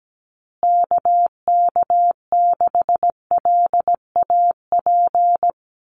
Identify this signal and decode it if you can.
Morse code, heard as a keyed tone, about 17 wpm, 700 Hz. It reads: KK6LAP
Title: USA Call Signs Repeated 2x 17wpm